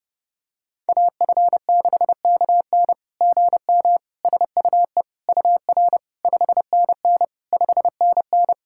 Morse code, heard as a keyed tone, about 30 wpm, 690 Hz. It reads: AF6KN GM SUE UR 5NN 5NN
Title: SOTA QSO Practice 30wpm